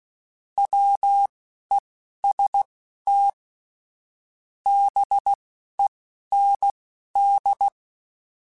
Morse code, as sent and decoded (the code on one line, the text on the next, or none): .-- . ... - / -... -. -..
WEST BEND